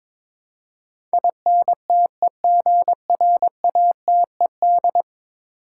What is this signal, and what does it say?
Morse code, heard as a keyed tone, about 22 wpm, 700 Hz. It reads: INTEGRATED
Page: morse-code-ninja-practice-22wpm